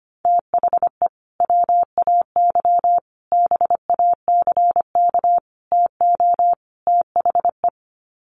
Morse code, heard as a keyed tone, about 25 wpm, 700 Hz. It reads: THE WAY BACK TO THE
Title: Sets of 5 Words Encore2 25wpm